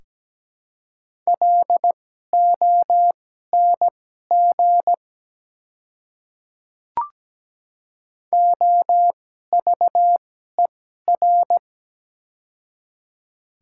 Morse code, heard as a keyed tone, about 17 wpm, spaced as for 12 wpm, 690 Hz. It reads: LONG OVER
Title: 2x Character Spacing - Top 100 Words - 17wpm